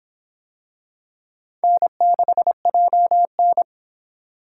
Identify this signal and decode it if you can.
Morse code, heard as a keyed tone, about 25 wpm, 700 Hz. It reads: N6JN